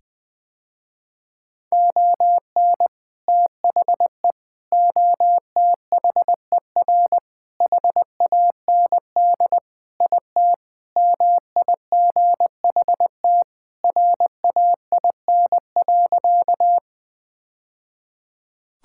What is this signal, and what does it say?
Morse code, heard as a keyed tone, about 20 wpm, 700 Hz. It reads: ON THE OTHER HAND IT MIGHT RAIN.
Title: Sentences with 3-Word Phrases - Spoken Beginning - 20wpm